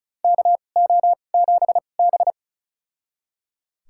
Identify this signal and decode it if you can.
Morse code, heard as a keyed tone, about 36 wpm, spaced as for 25 wpm, 700 Hz. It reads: KO7B
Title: USA Call Signs - 4 Characters - 2x Character Spacing - 35wpm